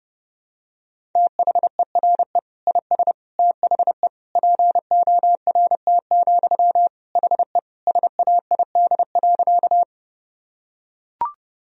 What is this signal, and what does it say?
Morse code, heard as a keyed tone, about 30 wpm, 700 Hz. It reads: THERE IS THE PORT, HE SAID.
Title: Sentences from Top 1000 Words - Review Part 1 - 30wpm